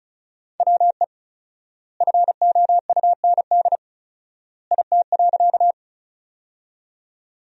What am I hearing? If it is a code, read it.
Morse code, heard as a keyed tone, about 36 wpm, 700 Hz. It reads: WE FOUND IT.
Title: 4x Word-Spacing - Sentences from Top 300 Words - 35wpm